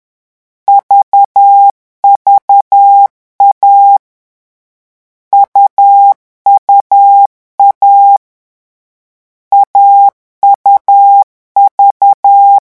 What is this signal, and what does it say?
Morse code, heard as a keyed tone, about 10 wpm, 800 Hz. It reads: VVA UUA AUV
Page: Leçon 3 - vitesse 10 mots minute